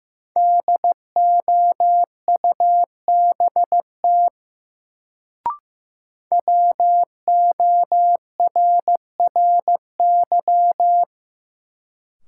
Morse code, heard as a keyed tone, about 15 wpm, 700 Hz. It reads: DOUBT WORRY
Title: Sets of 5 Words - New Word Drill - 15wpm